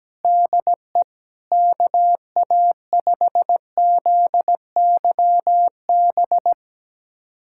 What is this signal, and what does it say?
Morse code, heard as a keyed tone, about 17 wpm, 700 Hz. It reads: DE KA5ZYB
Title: USA Callsign DE Callsign - 6 Characters - 17wpm